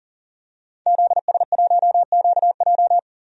Morse code, heard as a keyed tone, about 40 wpm, 700 Hz. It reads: ZS1QJ
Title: DXCC-100 Call Signs - 5 Characters - 40wpm